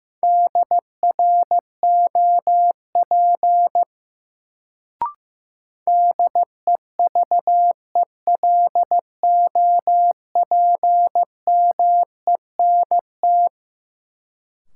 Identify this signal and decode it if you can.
Morse code, heard as a keyed tone, about 15 wpm, 700 Hz. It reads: DROP DEVELOPMENT
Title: Sentences from Top 1400 Words - New Word Drill Alt - 15wpm